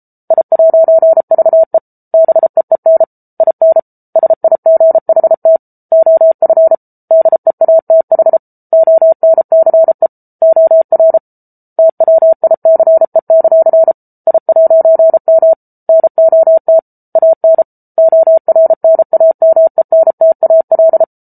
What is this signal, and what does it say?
Morse code, heard as a keyed tone, about 34 wpm, 650 Hz. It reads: I'VE BEEN IN SIGHT OF DEATH ONCE OR TWICE; I'M NOT AN ORNAMENTAL